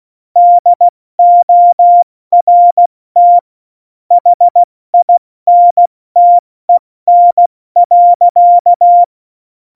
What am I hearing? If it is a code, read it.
Morse code, heard as a keyed tone, about 16 wpm, 700 Hz. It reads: DORT HINTEN.
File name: Morsekurs / Witze / Witz 13. November 2018, 70 F80